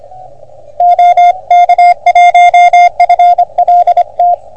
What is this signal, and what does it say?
Morse code, heard as a keyed tone, about 26 wpm, 720 Hz. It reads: OK1FLT
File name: OM3PA PILUP CUC 388